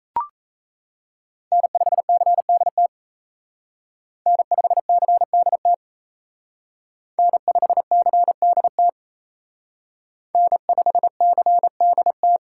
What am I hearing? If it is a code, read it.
Morse code, heard as a keyed tone, about 32 wpm, 700 Hz. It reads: N5CDT N5CDT N5CDT N5CDT
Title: Speed Racing - USA Call Signs 28wpm